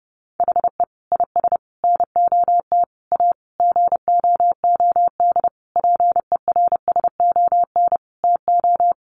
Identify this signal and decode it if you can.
Morse code, heard as a keyed tone, about 30 wpm, 700 Hz. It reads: HE IS NOT A GOOD PERSON TO
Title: Sentences from Top 1400 Words 30wpm